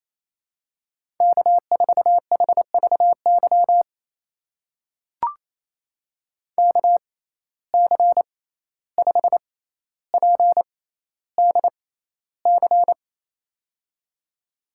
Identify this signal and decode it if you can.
Morse code, heard as a keyed tone, about 28 wpm, spaced as for 14 wpm, 700 Hz. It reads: K4HVY KC5PDC
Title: USA Call Signs - 6x Character Spacing - 28wpm